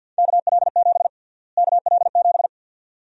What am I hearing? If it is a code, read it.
Morse code, heard as a keyed tone, about 52 wpm, 700 Hz. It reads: KL7 KL7
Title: USA Call Sign Prefixes - Repeated 2x - 50wpm